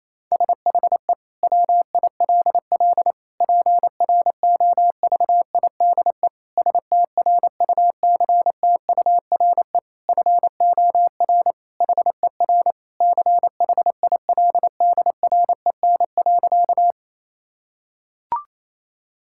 Morse code, heard as a keyed tone, about 27 wpm, 700 Hz. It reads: SHE WILL PROVIDE STRUCTURE FOR HER CHILDREN.
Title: Sentences from Top 1000 Words - Review Part 6 - 28wpm